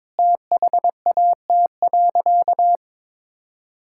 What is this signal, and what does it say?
Morse code, heard as a keyed tone, about 22 wpm, 700 Hz. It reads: THAT.
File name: practice / 22wpm / Sentences with 3-Word Phrases - 22wpm